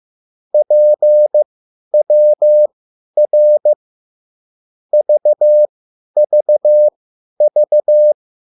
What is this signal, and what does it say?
Morse code, heard as a keyed tone, about 15 wpm, spaced as for 10 wpm, 600 Hz. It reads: PWR VVV
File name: QSL / CW lesson